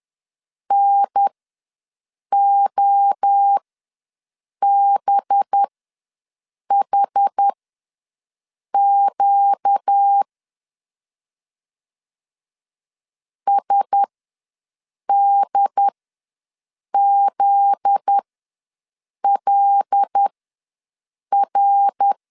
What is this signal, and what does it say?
Morse code, heard as a keyed tone, about 10 wpm, spaced as for 6 wpm, 800 Hz. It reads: NOBHQ SDZLR
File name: LEZIONI / 8 YJ45 / 8 Y ws 10wpm 3952